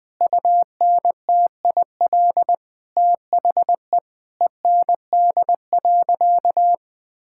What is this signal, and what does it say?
Morse code, heard as a keyed tone, about 20 wpm, 700 Hz. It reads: UNTIL THE END.
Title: Sentences from Top 1000 Words - Review Part 5 - 20wpm